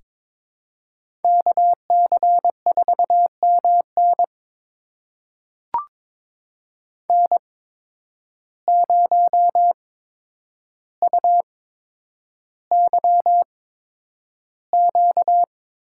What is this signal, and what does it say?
Morse code, heard as a keyed tone, about 23 wpm, spaced as for 10 wpm, 700 Hz. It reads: KC4MN N0UYQ
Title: USA Call Signs - 5 Characters - 8x Character Spacing - 22wpm